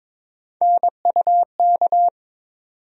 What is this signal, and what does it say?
Morse code, heard as a keyed tone, about 22 wpm, 700 Hz. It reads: NUK
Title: USA Call Sign Postfixes - 3 Letters 22wpm